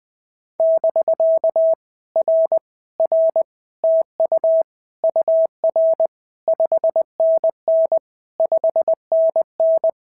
Morse code, heard as a keyed tone, about 20 wpm, 650 Hz. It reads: <BK> R R TU UR 5NN 5NN
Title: POTA QSO Practice - 20wpm